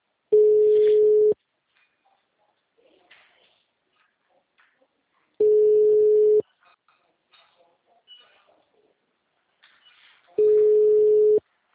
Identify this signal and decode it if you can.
Ringback tone (Europe, 425 Hz).
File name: out- -2008- - - .6259